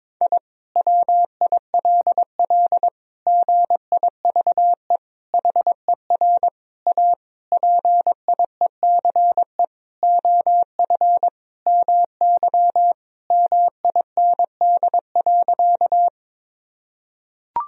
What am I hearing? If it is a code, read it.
Morse code, heard as a keyed tone, about 22 wpm, 700 Hz. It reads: I WILL GIVE HER A PIECE OF MY MIND.
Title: Sentences from Top 1000 Words - Review Part 2 - 22wpm